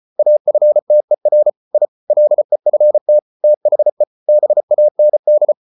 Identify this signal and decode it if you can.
Morse code, heard as a keyed tone, about 34 wpm, 600 Hz. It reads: AFTER I LEFT THE BAND